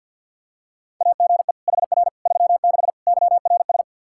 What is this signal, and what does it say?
Morse code, heard as a keyed tone, about 52 wpm, 700 Hz. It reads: AGE HR 36 YRS
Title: Practice QSO - Exchange 3 - More Info - 50wpm